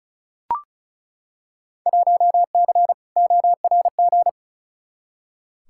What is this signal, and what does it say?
Morse code, heard as a keyed tone, about 36 wpm, 700 Hz. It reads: 1C ORG
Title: Field Day Exchanges - 35wpm